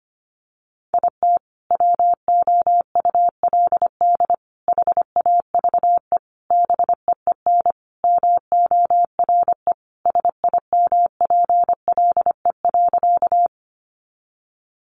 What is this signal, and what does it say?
Morse code, heard as a keyed tone, about 25 wpm, 700 Hz. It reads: IT WOULD HAVE BEEN MORE SIMPLE.
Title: Sentences from Top 400 Words 25wpm